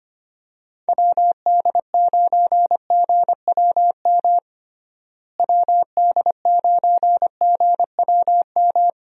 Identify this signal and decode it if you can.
Morse code, heard as a keyed tone, about 25 wpm, 700 Hz. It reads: WD9GWM WD9GWM
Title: USA Call Signs - 6 Characters - Repeated 2x - 25wpm